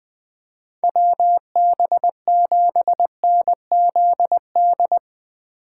Morse code, heard as a keyed tone, about 20 wpm, 700 Hz. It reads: WB7NZD
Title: USA Call Signs - 6 Characters - 4x Character Spacing - 20wpm